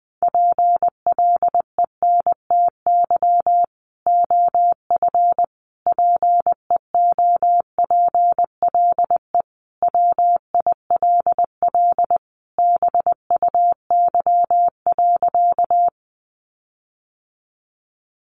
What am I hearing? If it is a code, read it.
Morse code, heard as a keyed tone, about 20 wpm, 700 Hz. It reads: PLENTY OF PEOPLE WILL BUY.
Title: Sentences from Top 1300 Words 20wpm